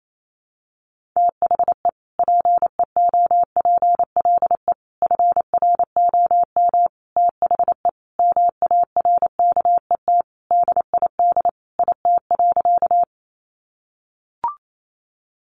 Morse code, heard as a keyed tone, about 28 wpm, 700 Hz. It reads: THE PEOPLE FROM THE MARKET DID IT.
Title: Sentences from Top 600 Words 28wpm